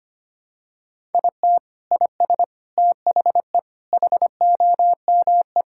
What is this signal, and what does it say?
Morse code, heard as a keyed tone, about 25 wpm, 700 Hz. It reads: IT IS THE HOME
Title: Sentences from Top 1000 Words - Review Part 5 - 25wpm